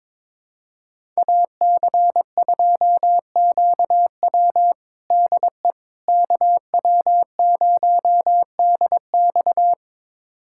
Morse code, heard as a keyed tone, about 22 wpm, 700 Hz. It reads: AC2QW DE KW0DX
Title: USA Callsign DE Callsign - 5 Characters - 22wpm